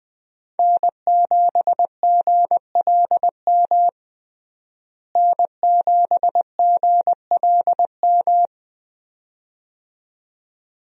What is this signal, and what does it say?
Morse code, heard as a keyed tone, about 21 wpm, 700 Hz. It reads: N7GLM N7GLM
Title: USA Call Signs - 5 Characters - Repeated 2x - 20wpm